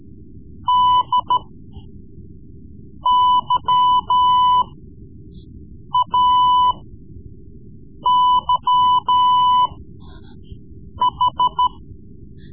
Morse code, heard as a keyed tone, about 14 wpm, spaced as for 5 wpm, 1000 Hz. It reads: DYAYH
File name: Morse Lessons / Morse Lessons / Tape-3B